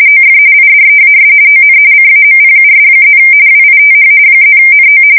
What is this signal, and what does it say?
RTTY, about 45.45 baud. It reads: OF IDEAS ON CAMPUS+ WITH +NO
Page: 170 Shift 2125/2295